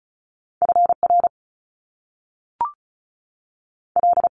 Morse code, heard as a keyed tone, about 36 wpm, 700 Hz. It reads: F R L
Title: Top Two Letter Combinations in Words 35wpm